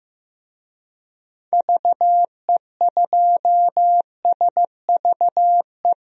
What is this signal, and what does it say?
Morse code, heard as a keyed tone, about 15 wpm, 700 Hz. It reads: VE2SVE
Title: DXCC-100 Call Signs - 6 Characters - 15wpm